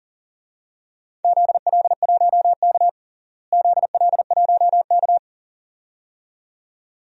Morse code, heard as a keyed tone, about 38 wpm, 700 Hz. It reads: ZL1K ZL1K
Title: DXCC-100 Call Signs - 4 Characters - Repeated 2x - 40wpm